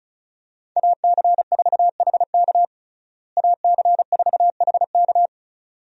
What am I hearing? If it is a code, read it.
Morse code, heard as a keyed tone, about 34 wpm, 700 Hz. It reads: AC4HK AC4HK
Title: USA Call Signs - 5 Characters - Repeated 2x - 35wpm